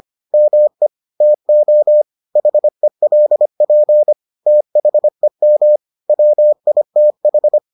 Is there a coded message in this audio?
Yes: Morse code from 0.3 to 7.6 s, heard as a keyed tone, about 25 wpm, 600 Hz. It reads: ME TO HELP THEM WITH